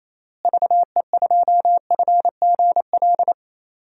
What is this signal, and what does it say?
Morse code, heard as a keyed tone, about 28 wpm, 700 Hz. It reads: VE2FGL
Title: DXCC-100 Call Signs - 6 Characters - Repeated 3x - 28wpm